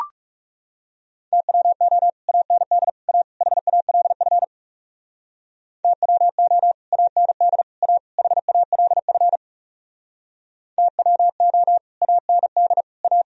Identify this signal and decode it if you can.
Morse code, heard as a keyed tone, about 38 wpm, 700 Hz. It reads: TWO AND A HALF TWO AND A HALF TWO AND A